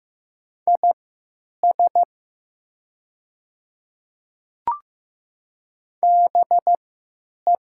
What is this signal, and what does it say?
Morse code, heard as a keyed tone, about 15 wpm, spaced as for 9 wpm, 700 Hz. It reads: IS BE
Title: 3x Character Spacing - Top 100 Words - 15wpm